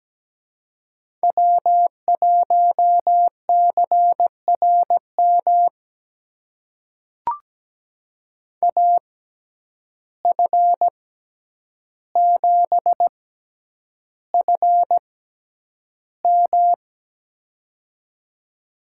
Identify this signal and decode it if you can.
Morse code, heard as a keyed tone, about 17 wpm, spaced as for 9 wpm, 700 Hz. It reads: W1CRM AF7FM
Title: USA Call Signs - 5 Characters - 6x Character Spacing - 17wpm